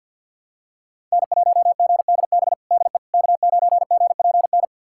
Morse code, heard as a keyed tone, about 52 wpm, 690 Hz. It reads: N1GDB DE K9GPN
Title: USA Callsign DE Callsign - 5 Characters - 50wpm